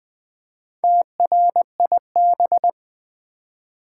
Morse code, heard as a keyed tone, about 20 wpm, 700 Hz. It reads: TRIB
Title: Build A Word - 2 Letters At A Time - 6-Letter Words - 20wpm